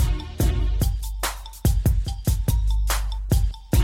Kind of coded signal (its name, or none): none